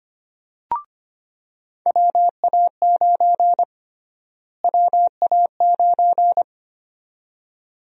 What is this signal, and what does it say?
Morse code, heard as a keyed tone, about 25 wpm, 700 Hz. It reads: WA9 WA9
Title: USA Call Sign Prefixes - Repeated 2x - 25wpm